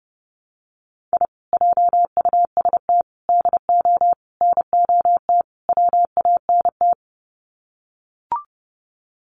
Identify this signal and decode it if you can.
Morse code, heard as a keyed tone, about 30 wpm, 700 Hz. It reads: I JUST DO NOT WANT
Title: Sets of 5 Words Encore2 30wpm